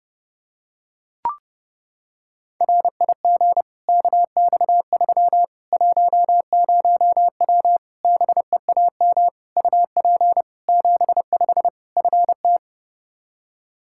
Morse code, heard as a keyed tone, about 32 wpm, 700 Hz. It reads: RIG KX3 10W BEAM UP 75 FT
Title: Practice QSO - Exchange 2 - Info - 30wpm